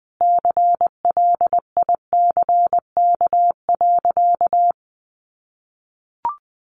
Morse code, heard as a keyed tone, about 20 wpm, 700 Hz. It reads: CLICK.
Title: Sentences from Top 1100 Words 20wpm